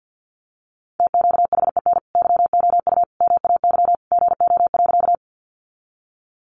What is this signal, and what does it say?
Morse code, heard as a keyed tone, about 50 wpm, 700 Hz. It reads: TQHEN YOU MAY GO.
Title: Sentences from Top 100 Words 50wpm